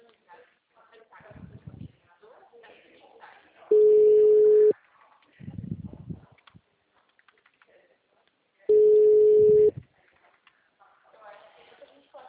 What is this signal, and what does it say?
Ringback tone (Europe, 425 Hz), with speech.